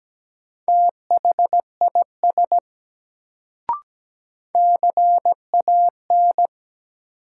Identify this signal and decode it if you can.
Morse code, heard as a keyed tone, about 17 wpm, 700 Hz. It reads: THIS CAN
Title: Sentences from Top 100 Words - New Word Drill - 17wpm